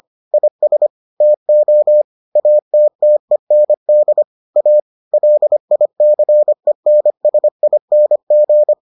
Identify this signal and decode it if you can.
Morse code, heard as a keyed tone, about 25 wpm, 600 Hz. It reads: IS TO ATTEND A LICENSING